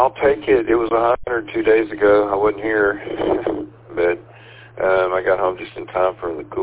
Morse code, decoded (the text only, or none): none